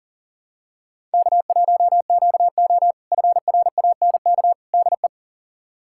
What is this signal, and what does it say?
Morse code, heard as a keyed tone, about 40 wpm, 700 Hz. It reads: K1QO FRANK DE
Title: K1USN SST QSO Practice - 40wpm